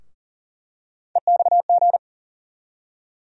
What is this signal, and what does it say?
Morse code, heard as a keyed tone, about 40 wpm, 700 Hz. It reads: EXG
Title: USA Call Sign Postfixes - 3 Letters 40wpm